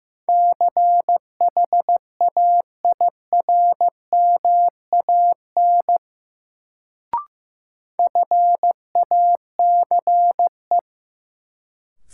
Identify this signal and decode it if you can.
Morse code, heard as a keyed tone, about 15 wpm, 700 Hz. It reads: CHAIRMAN FACE
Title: Sets of 3 Words - New Word Drill - 15wpm